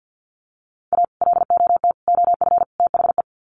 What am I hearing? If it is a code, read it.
Morse code, heard as a keyed tone, about 50 wpm, 700 Hz. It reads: A LOT OF THE
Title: Sets of 4 Words 50wpm